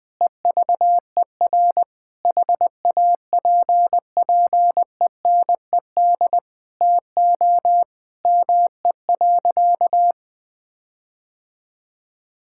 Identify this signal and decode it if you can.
Morse code, heard as a keyed tone, about 20 wpm, 700 Hz. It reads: EVER HAPPENED TO ME.
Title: Sentences with 8-Word Phrases - Spoken Beginning - 20wpm